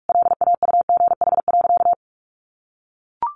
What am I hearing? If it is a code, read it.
Morse code, heard as a keyed tone, about 44 wpm, 700 Hz. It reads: LAUGH.
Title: Sentences from Top 1500 Words 45wpm